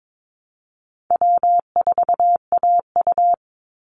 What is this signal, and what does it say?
Morse code, heard as a keyed tone, about 22 wpm, 700 Hz. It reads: W4AU
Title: Easy Call Signs - 22wpm